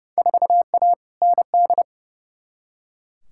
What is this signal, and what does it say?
Morse code, heard as a keyed tone, about 32 wpm, 700 Hz. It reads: 4A ND
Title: Field Day Exchanges Encore - 30wpm